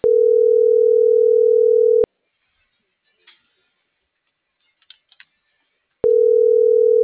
Ringback tone (North America), 440 + 480 Hz.